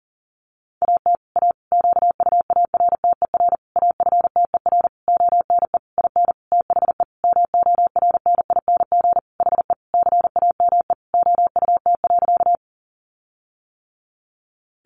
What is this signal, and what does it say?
Morse code, heard as a keyed tone, about 40 wpm, 700 Hz. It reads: AT A QUARTER AFTER ONE IN THE MORNING HE CAME OUT.
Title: Sentences from Top 1300 Words 40wpm